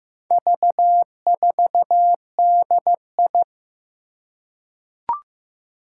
Morse code, heard as a keyed tone, about 15 wpm, 700 Hz. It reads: V4DI
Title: Four Challenging Letter-Number Combinations 15wpm